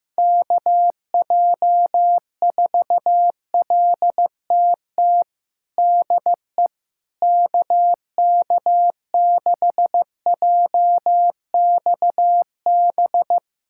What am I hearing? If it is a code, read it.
Morse code, heard as a keyed tone, about 15 wpm, 700 Hz. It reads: KJ4LTT DE KK6JXB
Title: USA Callsign DE Callsign 15wpm